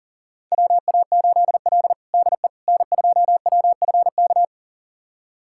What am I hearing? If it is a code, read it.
Morse code, heard as a keyed tone, about 40 wpm, 700 Hz. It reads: WA8L DE N2WFK
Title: USA Callsign DE Callsign 40wpm